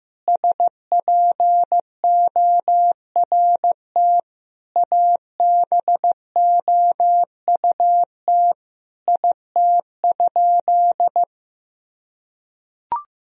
Morse code, heard as a keyed tone, about 15 wpm, 700 Hz. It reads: SPORT ABOUT IT?
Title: Sentences from Top 1000 Words - Review Part 4 - 15wpm